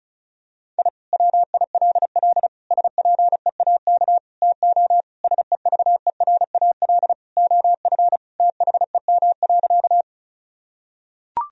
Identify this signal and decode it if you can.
Morse code, heard as a keyed tone, about 36 wpm, 700 Hz. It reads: I WILL SPEAK TO SEVERAL OF THEM.
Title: Sentences from Top 1000 Words - Review Part 2 - 35wpm